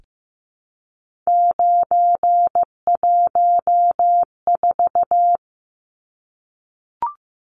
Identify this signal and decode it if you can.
Morse code, heard as a keyed tone, about 15 wpm, 700 Hz. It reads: 914
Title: Numbers - 3-Digit 15wpm